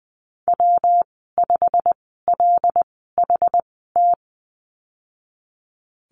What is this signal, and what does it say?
Morse code, heard as a keyed tone, about 20 wpm, spaced as for 15 wpm, 700 Hz. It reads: W5LHT
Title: 193 Lesson 31g - Letter v - Call Signs 20wpm